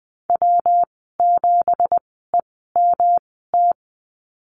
Morse code, heard as a keyed tone, about 19 wpm, spaced as for 14 wpm, 700 Hz. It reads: W7EMT
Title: 179 Lesson 29g - Slash - Call Signs 20wpm